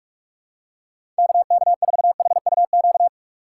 Morse code, heard as a keyed tone, about 43 wpm, 700 Hz. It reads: KK4HUQ
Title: USA Call Signs - 6 Characters - 4x Character Spacing - 45wpm